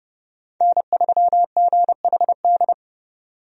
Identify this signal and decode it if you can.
Morse code, heard as a keyed tone, about 30 wpm, 700 Hz. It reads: N3GHD
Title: USA Call Signs - 5 Characters - 30wpm